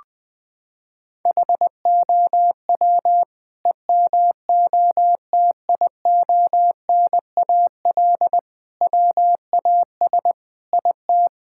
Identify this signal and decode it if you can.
Morse code, heard as a keyed tone, about 20 wpm, 700 Hz. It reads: HOW EMOTIONAL WAS IT
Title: Sentences from Top 1000 Words - Review Part 5 - 20wpm